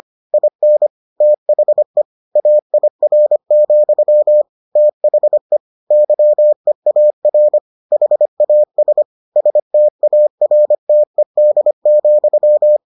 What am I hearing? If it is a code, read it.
Morse code, heard as a keyed tone, about 25 wpm, 600 Hz. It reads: IN THE AIR, THE YEAR HAS STARTED,